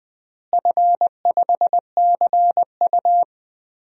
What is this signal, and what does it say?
Morse code, heard as a keyed tone, about 20 wpm, 700 Hz. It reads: F5CU
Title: DXCC-100 Call Signs - 4 Characters - Repeated 3x - 20wpm